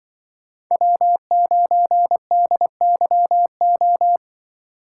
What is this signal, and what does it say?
Morse code, heard as a keyed tone, about 25 wpm, 700 Hz. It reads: W9DYO
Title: Speed Racing - USA 1x3 Call Signs 20wpm